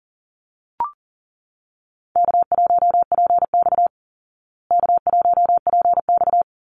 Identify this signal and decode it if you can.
Morse code, heard as a keyed tone, about 40 wpm, 700 Hz. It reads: K1PX K1PX
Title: Call Signs Repeated 3x - 40wpm